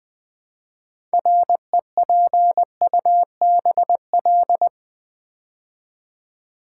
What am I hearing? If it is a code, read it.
Morse code, heard as a keyed tone, about 20 wpm, 700 Hz. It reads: REPUBL
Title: Build A Word - 2 Letters At A Time - 8-Letter Words - 20wpm